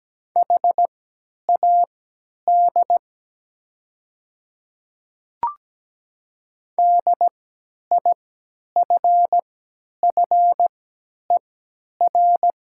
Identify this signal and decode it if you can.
Morse code, heard as a keyed tone, about 17 wpm, spaced as for 10 wpm, 700 Hz. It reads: HAD DIFFER
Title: 3x Character Spacing - Top 500 Words - 17wpm